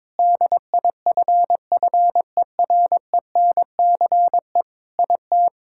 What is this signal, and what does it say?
Morse code, heard as a keyed tone, about 22 wpm, 700 Hz. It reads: DIFFERENCE IT
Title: Sentences from Top 1000 Words - Review Part 5 - 22wpm